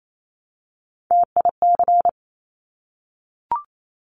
Morse code, heard as a keyed tone, about 28 wpm, 700 Hz. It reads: TIC
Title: Top Three Letter Combinations in Words 28wpm